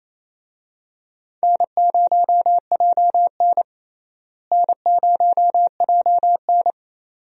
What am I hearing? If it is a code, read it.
Morse code, heard as a keyed tone, about 28 wpm, 700 Hz. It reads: N0JN N0JN
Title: USA Call Signs - 4 Characters - Repeated 2x - 28wpm